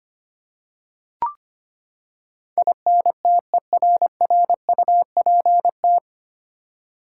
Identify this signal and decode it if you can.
Morse code, heard as a keyed tone, about 25 wpm, 700 Hz. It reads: INTERRUPT
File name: Nine Letter Words - 25wpm